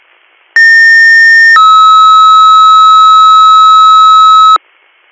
Two-tone page, heard as a one-second tone, then a three-second tone, about 1800 then 1290 Hz. It is heard from 0.6 to 4.6 s.